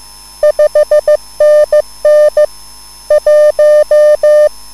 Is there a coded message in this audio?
Yes: Morse code from 0.4 to 4.5 s, heard as a keyed tone, about 15 wpm, 600 Hz. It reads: 5NN 1